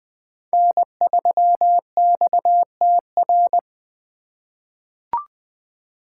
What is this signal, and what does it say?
Morse code, heard as a keyed tone, about 20 wpm, 700 Hz. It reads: N3XTR
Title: USA Call Signs - 5 Characters - 4x Character Spacing - 20wpm